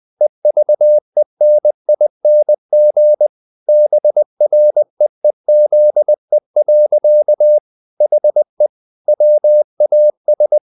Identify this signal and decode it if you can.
Morse code, heard as a keyed tone, about 20 wpm, 600 Hz. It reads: EVENING BREEZE. HE WAS
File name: the old man and the sea / the old man and the sea part 1 20wpm